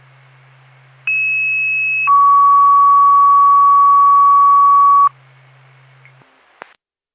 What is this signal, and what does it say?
Two-tone page, heard as a one-second tone, then a three-second tone, about 2600 then 1130 Hz. It is heard from 1.1 to 5.1 s.